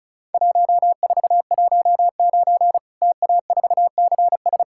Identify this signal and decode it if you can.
Morse code, heard as a keyed tone, about 36 wpm, 700 Hz. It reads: 1419 TA4CS
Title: Worked All Europe DX Contest - QTC Exchange - 35wpm